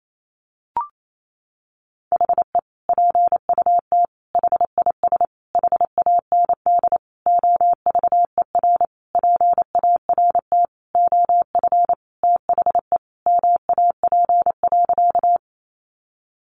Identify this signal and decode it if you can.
Morse code, heard as a keyed tone, about 28 wpm, 700 Hz. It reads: HE PUT HIS HAND OVER PART OF THE MAP.
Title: Sentences from Top 400 Words 28wpm